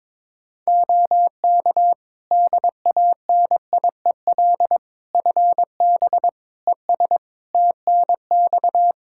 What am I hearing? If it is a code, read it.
Morse code, heard as a keyed tone, about 22 wpm, 700 Hz. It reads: OK DANIEL FB ES TNX
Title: Practice QSO - Exchange 2 - Info - 22wpm